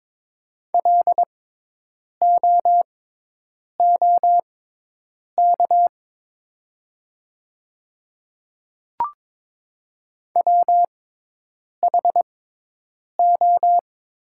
Morse code, heard as a keyed tone, about 22 wpm, spaced as for 8 wpm, 700 Hz. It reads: LOOK WHO
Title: 6x Character Spacing - Top 100 Words - 22wpm